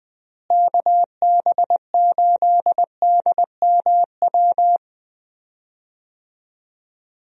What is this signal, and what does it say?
Morse code, heard as a keyed tone, about 20 wpm, 700 Hz. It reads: KB8DMW
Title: USA Call Signs - 6 Characters - Repeated 2x - 20wpm